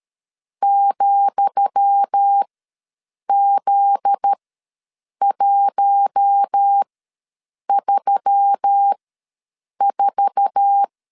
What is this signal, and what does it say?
Morse code, heard as a keyed tone, about 13 wpm, spaced as for 7 wpm, 800 Hz. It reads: ,Z134
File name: LEZIONI / 10 SIGNS / 10D COMA ws 12wpm 3618